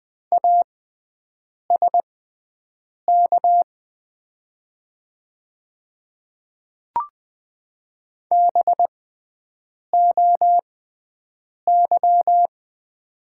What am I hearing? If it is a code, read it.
Morse code, heard as a keyed tone, about 20 wpm, spaced as for 7 wpm, 700 Hz. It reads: ASK BOY
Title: 6x Character Spacing - Top 200 Words - 20wpm